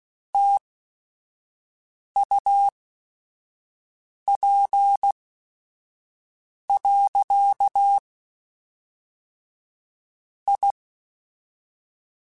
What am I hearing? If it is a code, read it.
Morse code, heard as a keyed tone, about 16 wpm, spaced as for 5 wpm, 800 Hz. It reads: TUP. I
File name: QSOs / QSOs-mp3s / WD0BGP 1@5